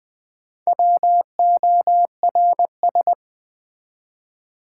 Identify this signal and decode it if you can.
Morse code, heard as a keyed tone, about 20 wpm, 700 Hz. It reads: WORS
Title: Build A Word - 1 Letter At A Time - 3 to 5-Letter Words - 20wpm